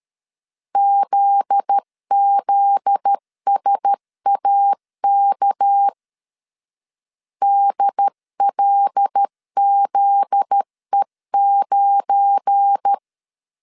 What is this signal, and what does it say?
Morse code, heard as a keyed tone, about 13 wpm, 800 Hz. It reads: ZZSAK DLZE9
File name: LEZIONI / 7 GF9Z / 7 GF9Z 12wpm 2147